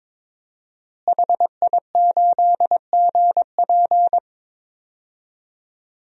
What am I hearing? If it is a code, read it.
Morse code, heard as a keyed tone, about 22 wpm, 700 Hz. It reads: HI8GP
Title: DXCC-100 Call Signs - 5 Characters - Repeated 2x - 22wpm